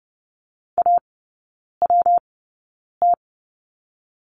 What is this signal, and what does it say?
Morse code, heard as a keyed tone, about 30 wpm, spaced as for 9 wpm, 700 Hz. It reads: AWT
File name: Lesson 19e - Letter w - Three Characters 30wpm